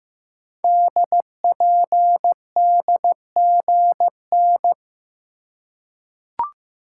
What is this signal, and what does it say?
Morse code, heard as a keyed tone, about 15 wpm, 700 Hz. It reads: DPDGN
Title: Code Groups - 5 Letters - 15wpm